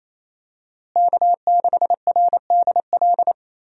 Morse code, heard as a keyed tone, about 28 wpm, 700 Hz. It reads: K6RDL
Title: USA Call Signs - 5 Characters - Repeated 3x - 28wpm